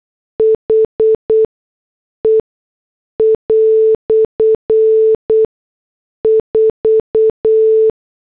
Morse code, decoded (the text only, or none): H E " 4